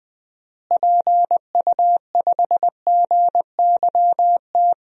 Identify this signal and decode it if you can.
Morse code, heard as a keyed tone, about 20 wpm, 700 Hz. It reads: PU5GYT
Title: DXCC-100 Call Signs - 6 Characters - Repeated 2x - 20wpm